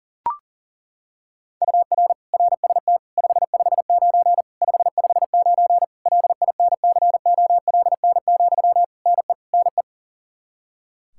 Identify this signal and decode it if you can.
Morse code, heard as a keyed tone, about 40 wpm, 700 Hz. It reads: UR RST 559 559 LINCOLN, NE NE
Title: Practice QSO - Exchange 1 - Beginning - 40wpm